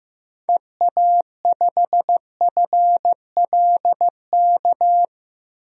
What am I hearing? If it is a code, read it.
Morse code, heard as a keyed tone, about 15 wpm, 700 Hz. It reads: EA5FLK
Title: DXCC-100 Call Signs - 6 Characters - Repeated 3x - 15wpm